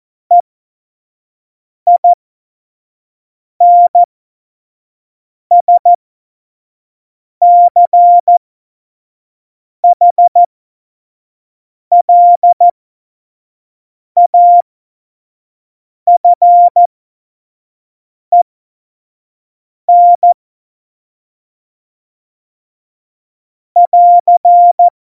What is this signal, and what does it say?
Morse code, heard as a keyed tone, about 14 wpm, spaced as for 5 wpm, 700 Hz. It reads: EINSCHLAFEN <AR>
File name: Morsekurs / Witze / Witz 09. Oktober 2018, 25 F70